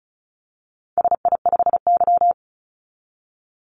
Morse code, heard as a keyed tone, about 34 wpm, 700 Hz. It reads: SI5Y
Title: Call Signs - 35wpm